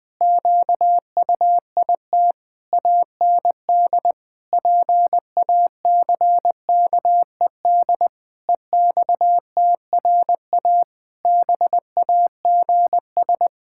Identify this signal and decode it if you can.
Morse code, heard as a keyed tone, about 20 wpm, 700 Hz. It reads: QUIT AND PACKED EXTRA BAGS
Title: Pangram Challenge v2 - 20wpm